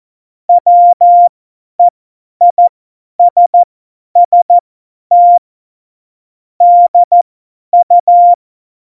Morse code, heard as a keyed tone, about 14 wpm, spaced as for 10 wpm, 700 Hz. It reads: WEISST DU
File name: Morsekurs / Witze / Witz 05. Dezember 2017, 50 F70